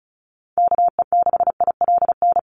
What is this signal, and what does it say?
Morse code, heard as a keyed tone, about 34 wpm, 700 Hz. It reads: KE6ILN
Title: USA Call Signs 35wpm